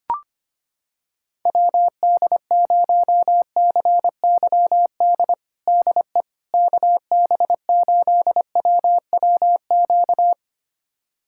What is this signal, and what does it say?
Morse code, heard as a keyed tone, about 25 wpm, 700 Hz. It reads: WD0CYD DE KB8WWQ
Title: USA Callsign DE Callsign - 6 Characters - 25wpm